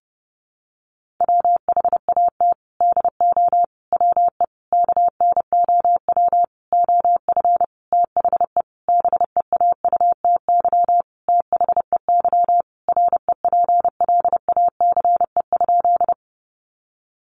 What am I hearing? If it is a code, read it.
Morse code, heard as a keyed tone, about 30 wpm, 700 Hz. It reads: WHAT DO WE KNOW OF THE BEAUTY THEY REPLACE?
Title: Sentences from Top 1400 Words 30wpm